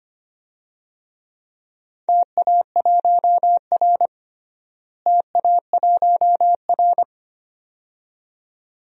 Morse code, heard as a keyed tone, about 25 wpm, 700 Hz. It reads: TA1R TA1R
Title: DXCC-100 Call Signs - 4 Characters - Repeated 2x - 25wpm